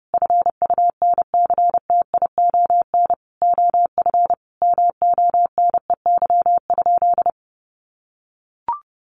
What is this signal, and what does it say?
Morse code, heard as a keyed tone, about 30 wpm, 700 Hz. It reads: FUNCTION OF MONEY?
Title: Sentences from Top 1100 Words 30wpm